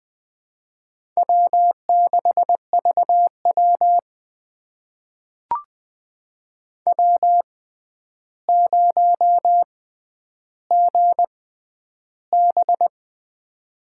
Morse code, heard as a keyed tone, about 20 wpm, spaced as for 10 wpm, 700 Hz. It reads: W6VW W0GB
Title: USA Call Signs - 4 Characters - 6x Character Spacing - 20wpm